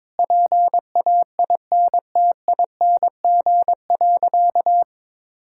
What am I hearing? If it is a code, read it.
Morse code, heard as a keyed tone, about 22 wpm, 700 Hz. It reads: PAINTING.
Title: Sentences from Top 1000 Words - Review Part 6 - 22wpm